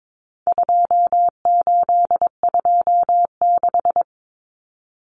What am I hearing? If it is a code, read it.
Morse code, heard as a keyed tone, about 22 wpm, 700 Hz. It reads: 2826
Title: Numbers - 4-Digit 22wpm